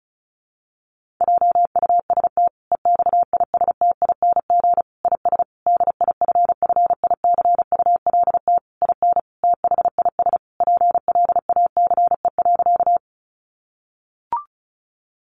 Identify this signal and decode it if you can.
Morse code, heard as a keyed tone, about 34 wpm, 700 Hz. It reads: JUST EXISTING IS DIFFICULT IN THIS PLACE.
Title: Sentences from Top 1500 Words 35wpm